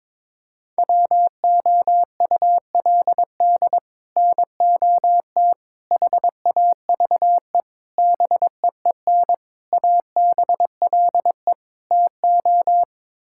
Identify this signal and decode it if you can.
Morse code, heard as a keyed tone, about 22 wpm, 700 Hz. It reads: WOULD NOT HAVE BEEN ABLE TO
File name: practice / 22wpm / Sets of 6 Words - 22wpm